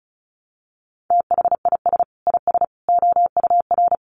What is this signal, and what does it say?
Morse code, heard as a keyed tone, about 34 wpm, 700 Hz. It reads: THIS IS OUR
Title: Sentences from Top 1000 Words 35wpm